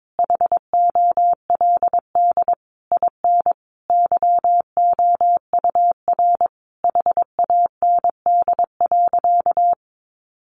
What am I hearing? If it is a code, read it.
Morse code, heard as a keyed tone, about 22 wpm, 700 Hz. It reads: HOLD IN YOUR HAND.
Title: Sentences from Top 400 Words 22wpm